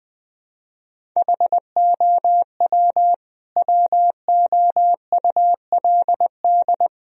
Morse code, heard as a keyed tone, about 20 wpm, 700 Hz. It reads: HOW WOULD
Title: Sentences from Top 1000 Words - Review Part 1 - 20wpm